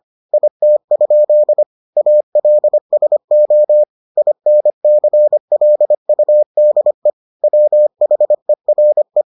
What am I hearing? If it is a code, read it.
Morse code, heard as a keyed tone, about 25 wpm, 600 Hz. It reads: IT? ALSO INCLUDE WHERE